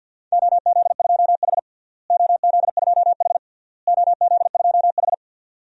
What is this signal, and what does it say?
Morse code, heard as a keyed tone, about 50 wpm, 700 Hz. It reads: OZ2H OZ2H OZ2H
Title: DXCC-100 Call Signs - 4 Characters - Repeated 3x - 50wpm